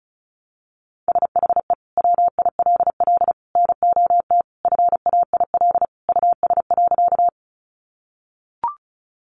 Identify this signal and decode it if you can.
Morse code, heard as a keyed tone, about 34 wpm, 700 Hz. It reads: SHE WILL NOT FAIL US.
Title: Sentences from Top 1100 Words 35wpm